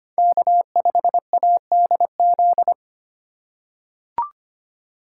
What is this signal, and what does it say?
Morse code, heard as a keyed tone, about 25 wpm, 700 Hz. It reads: K5ADZ
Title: USA Call Signs - 5 Characters - 2x Character Spacing - 25wpm